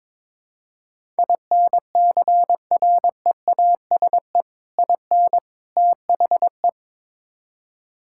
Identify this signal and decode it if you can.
Morse code, heard as a keyed tone, about 22 wpm, 700 Hz. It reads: INCREASE IN THE
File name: practice / 22wpm / Build A Phrase - 5-Word Phrases - 22wpm